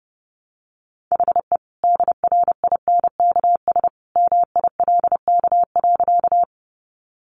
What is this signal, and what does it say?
Morse code, heard as a keyed tone, about 30 wpm, 700 Hz. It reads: HE DRINKS MILK.
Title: Easy Phrases 30wpm